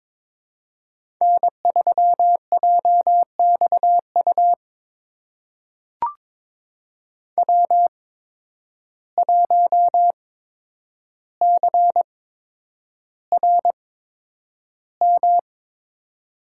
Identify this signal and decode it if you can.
Morse code, heard as a keyed tone, about 22 wpm, spaced as for 9 wpm, 700 Hz. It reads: N3JXU W1CRM
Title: USA Call Signs - 5 Characters - 8x Character Spacing - 22wpm